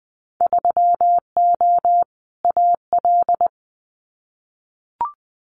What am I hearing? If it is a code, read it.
Morse code, heard as a keyed tone, about 20 wpm, 700 Hz. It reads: 3O AL
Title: Winter Field Day Exchanges 20wpm